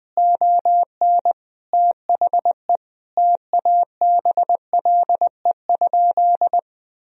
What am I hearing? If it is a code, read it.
Morse code, heard as a keyed tone, about 20 wpm, 700 Hz. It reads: ON THE TABLE?
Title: Sentences from Top 1000 Words - Review Part 4 - 20wpm